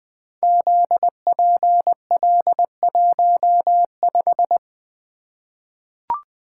Morse code, heard as a keyed tone, about 20 wpm, 700 Hz. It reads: ZPL15
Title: Code Groups - 5 Letter:Numbers - 20wpm